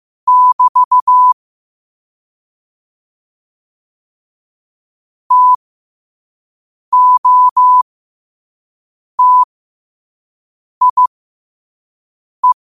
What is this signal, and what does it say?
Morse code, heard as a keyed tone, about 15 wpm, spaced as for 5 wpm, 1000 Hz. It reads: <BT> TOTIE
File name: qotdfiles / 5wpm